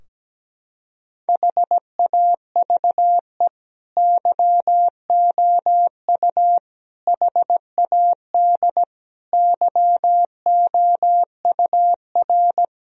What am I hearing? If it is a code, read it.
Morse code, heard as a keyed tone, about 17 wpm, 700 Hz. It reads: HAVE YOU HAD YOUR
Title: Sentences from Top 1000 Words - Review Part 3 - 17wpm